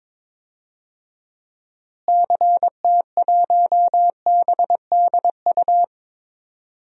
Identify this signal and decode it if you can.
Morse code, heard as a keyed tone, about 22 wpm, 700 Hz. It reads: CT1BDU
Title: DXCC-100 Call Signs - 6 Characters - Repeated 2x - 22wpm